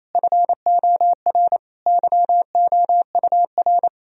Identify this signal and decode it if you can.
Morse code, heard as a keyed tone, about 27 wpm, 700 Hz. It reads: FOR YOUR
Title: Sentences from Top 1000 Words - Review Part 4 - 28wpm